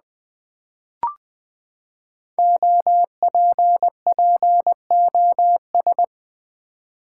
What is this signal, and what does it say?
Morse code, heard as a keyed tone, about 20 wpm, 700 Hz. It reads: OPPOS
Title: Build A Word - 5 Letters At A Time - 10-Letter Words - 20wpm